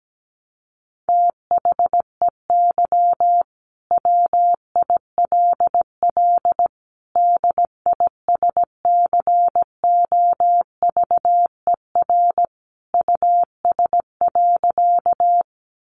Morse code, heard as a keyed tone, about 17 wpm, 700 Hz. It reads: THEY WILL DISCOVER US.
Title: Sentences from Top 1100 Words 17wpm